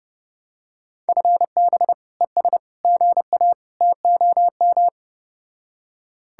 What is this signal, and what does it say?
Morse code, heard as a keyed tone, about 32 wpm, 700 Hz. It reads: FB ES GA TOM